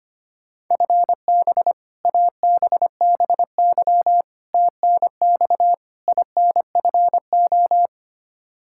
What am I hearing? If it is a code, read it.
Morse code, heard as a keyed tone, about 25 wpm, 700 Hz. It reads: FB ABBY TNX INFO
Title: Practice QSO - Exchange 3 - More Info - 25wpm